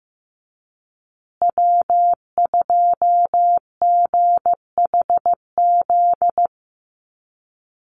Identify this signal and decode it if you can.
Morse code, heard as a keyed tone, about 15 wpm, 700 Hz. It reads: W2GHZ
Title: USA Call Signs Repeated 2x 15wpm